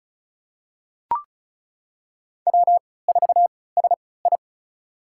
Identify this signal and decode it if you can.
Morse code, heard as a keyed tone, about 34 wpm, spaced as for 20 wpm, 700 Hz. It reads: W4SI
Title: USA Call Signs - 4 Characters - 3x Character Spacing - 35wpm